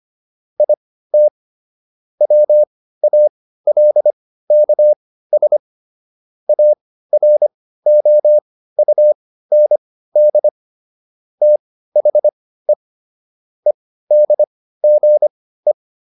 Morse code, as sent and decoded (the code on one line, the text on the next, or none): .. - / .-- .- .-.. -.- ... / .- .-. --- ..- -. -.. / - .... / . -.. --. .
IT WALKS AROUND THE EDGE